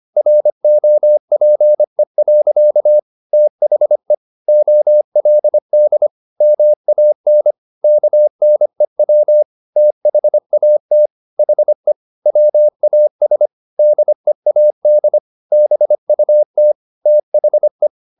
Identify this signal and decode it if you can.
Morse code, heard as a keyed tone, about 25 wpm, 600 Hz. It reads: ROPE. THE OLD MAN KNEW THAT HE WAS DEAD BUT THE